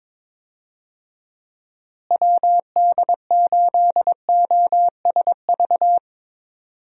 Morse code, heard as a keyed tone, about 22 wpm, 700 Hz. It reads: WD8OSV
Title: USA Call Signs - 6 Characters - Repeated 3x - 22wpm